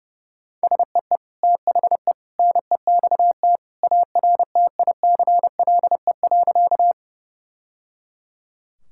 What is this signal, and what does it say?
Morse code, heard as a keyed tone, about 30 wpm, 700 Hz. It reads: SEE THE NEXT ARTICLE.
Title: Sentences from Top 1000 Words - Review Part 5 - 30wpm